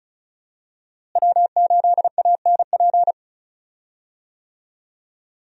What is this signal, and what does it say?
Morse code, heard as a keyed tone, about 36 wpm, 700 Hz. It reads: W8ANP
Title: USA Call Signs - 5 Characters - Repeated 2x - 35wpm